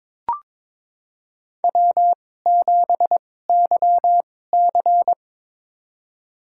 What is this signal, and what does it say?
Morse code, heard as a keyed tone, about 22 wpm, spaced as for 16 wpm, 700 Hz. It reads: W7YC
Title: USA Call Signs - 2x Character Spacing - 22wpm